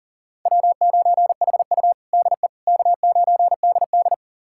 Morse code, heard as a keyed tone, about 37 wpm, 700 Hz. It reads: W9HU DE K9DD
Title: USA Callsign DE Callsign - 4 Characters - 40wpm